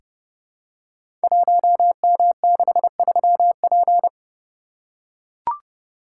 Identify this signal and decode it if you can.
Morse code, heard as a keyed tone, about 30 wpm, 700 Hz. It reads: 1M63P
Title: Code Groups - 5 Letter:Numbers - 30wpm